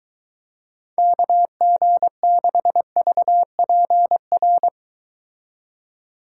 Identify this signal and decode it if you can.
Morse code, heard as a keyed tone, about 23 wpm, 700 Hz. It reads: KG6VPR